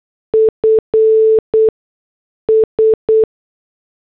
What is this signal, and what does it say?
Morse code, heard as a keyed tone, about 8 wpm, 440 Hz. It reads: F S
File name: morse / headlines